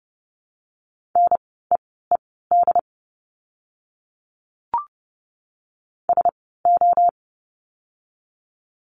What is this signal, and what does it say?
Morse code, heard as a keyed tone, about 30 wpm, spaced as for 17 wpm, 700 Hz. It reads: NEED SO
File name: course / 30:10wpm / 230 Lesson 37f - Letter x - Words 30wpm